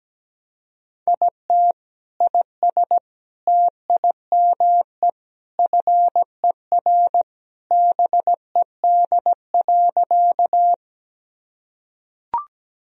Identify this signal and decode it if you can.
Morse code, heard as a keyed tone, about 17 wpm, 700 Hz. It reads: IT IS TIME FER BED.
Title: Sentences from Top 500 Words and Abbreviations 17wpm